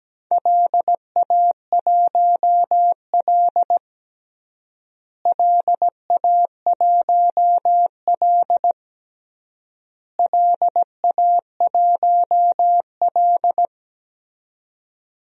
Morse code, heard as a keyed tone, about 17 wpm, 700 Hz. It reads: LA1L LA1L LA1L
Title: DXCC-100 Call Signs - 4 Characters - Repeated 3x - 17wpm